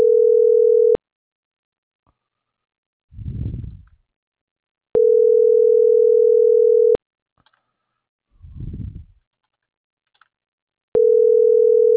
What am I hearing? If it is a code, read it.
Ringback tone (North America), 440 + 480 Hz.